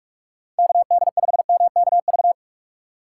Morse code, heard as a keyed tone, about 45 wpm, 700 Hz. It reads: KD5MKV
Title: USA Call Signs - 6 Characters - 2x Character Spacing - 45wpm